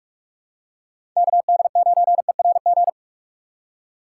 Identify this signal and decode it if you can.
Morse code, heard as a keyed tone, about 41 wpm, 700 Hz. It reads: KD9ERG
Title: USA Call Signs - 6 Characters - 3x Character Spacing - 45wpm